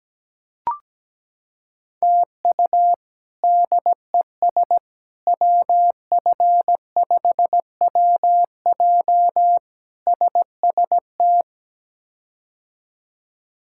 Morse code, heard as a keyed tone, about 17 wpm, 700 Hz. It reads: TU DES WF5WJ SST
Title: K1USN SST QSO Practice - 17wpm